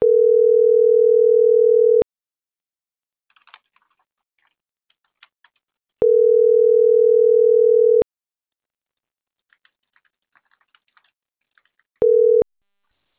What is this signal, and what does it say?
Ringback tone (North America), 440 + 480 Hz.